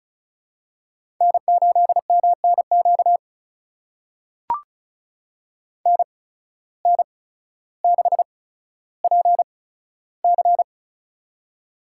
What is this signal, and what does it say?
Morse code, heard as a keyed tone, about 36 wpm, spaced as for 15 wpm, 700 Hz. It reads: N8MNQ NN6PC
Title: USA Call Signs - 5 Characters - 8x Character Spacing - 35wpm